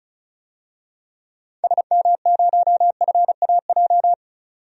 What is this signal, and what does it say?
Morse code, heard as a keyed tone, about 36 wpm, 700 Hz. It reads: SM0FAJ
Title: DXCC-100 Call Signs - 6 Characters - 35wpm